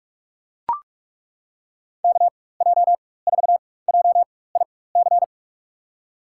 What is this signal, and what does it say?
Morse code, heard as a keyed tone, about 42 wpm, spaced as for 20 wpm, 700 Hz. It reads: KJ4JIC
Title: USA Call Signs - 6 Characters - 4x Character Spacing - 45wpm